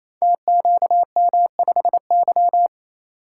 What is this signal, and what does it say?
Morse code, heard as a keyed tone, about 29 wpm, 700 Hz. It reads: TQM5Y
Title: Code Groups - 5 Letter:Numbers - 28wpm